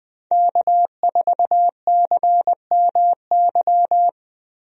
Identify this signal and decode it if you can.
Morse code, heard as a keyed tone, about 20 wpm, 700 Hz. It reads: K4CMY
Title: USA Call Signs - 5 Characters - Repeated 2x - 20wpm